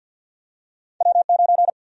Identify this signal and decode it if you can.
Morse code, heard as a keyed tone, about 53 wpm, 700 Hz. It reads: W9
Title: USA Call Sign Prefixes - 50wpm